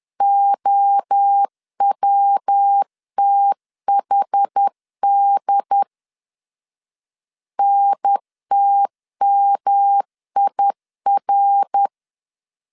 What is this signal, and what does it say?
Morse code, heard as a keyed tone, about 10 wpm, 800 Hz. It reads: OWTHD NTMIR
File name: CW / DET / 1 3 1500C S B 10wpm 3335